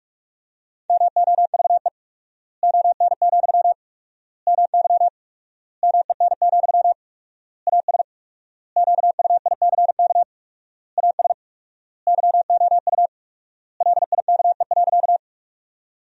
Morse code, heard as a keyed tone, about 47 wpm, 700 Hz. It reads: MOVE ON, MY MEN, AS QUICK AS YOU LIKE.
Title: 4x Word-Spacing - Sentences from Top 500 Words - 45wpm